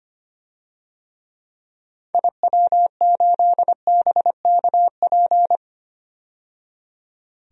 Morse code, heard as a keyed tone, about 25 wpm, 700 Hz. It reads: IW8BKP
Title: DXCC-100 Call Signs - 6 Characters - 25wpm